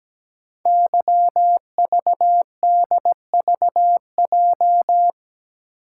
Morse code, heard as a keyed tone, about 17 wpm, 700 Hz. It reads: YVDVJ
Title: Code Groups - 5 Letters - 17wpm